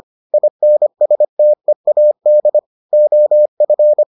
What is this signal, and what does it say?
Morse code, heard as a keyed tone, about 25 wpm, 600 Hz. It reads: INSTEAD OF